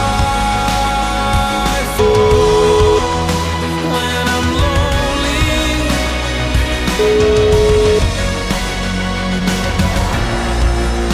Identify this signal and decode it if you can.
Ringback tone (Europe, 425 Hz), with noise.